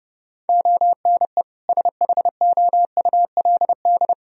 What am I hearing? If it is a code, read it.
Morse code, heard as a keyed tone, about 31 wpm, 700 Hz. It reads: ONE SHOULD
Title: Sentences from Top 1000 Words - Review Part 4 - 30wpm